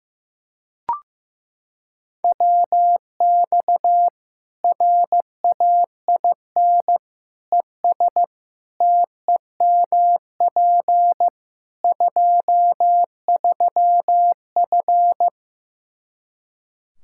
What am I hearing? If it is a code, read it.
Morse code, heard as a keyed tone, about 15 wpm, 700 Hz. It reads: WX RAIN ES TEMP 23F
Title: K1USN SST QSO Practice 15wpm